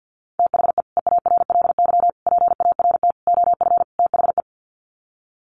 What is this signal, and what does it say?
Morse code, heard as a keyed tone, about 50 wpm, 700 Hz. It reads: THE EARLY PART OF THE
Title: Sets of 5 Words Encore2 50wpm